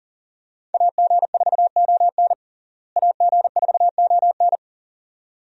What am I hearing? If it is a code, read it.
Morse code, heard as a keyed tone, about 37 wpm, 700 Hz. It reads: AG4ON AG4ON
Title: USA Call Signs - 5 Characters - Repeated 2x - 40wpm